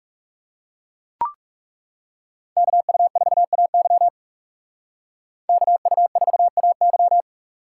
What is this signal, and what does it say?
Morse code, heard as a keyed tone, about 38 wpm, 700 Hz. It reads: KU4AY KU4AY